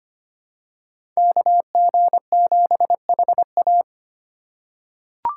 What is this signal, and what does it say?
Morse code, heard as a keyed tone, about 25 wpm, 700 Hz. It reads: KG7HA
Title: USA Call Signs - 5 Characters - 3x Character Spacing - 25wpm